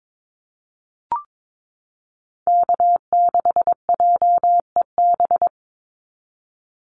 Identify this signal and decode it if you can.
Morse code, heard as a keyed tone, about 22 wpm, 690 Hz. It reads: K6JEB
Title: Call Signs Repeated 3x - 22wpm